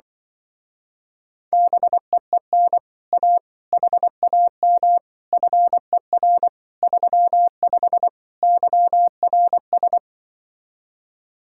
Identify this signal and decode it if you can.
Morse code, heard as a keyed tone, about 24 wpm, 700 Hz. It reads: BEEN A HAM FER 35 YRS
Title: Speed-Racing Common QSO Phrases - 20wpm